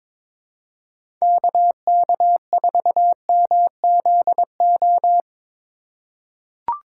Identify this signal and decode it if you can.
Morse code, heard as a keyed tone, about 22 wpm, 700 Hz. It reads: KK4MZO
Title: USA Call Signs - 6 Characters - 4x Character Spacing - 22wpm